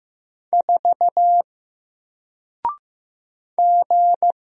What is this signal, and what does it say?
Morse code, heard as a keyed tone, about 15 wpm, 700 Hz. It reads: 4 G
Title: Single Letter-Number 15wpm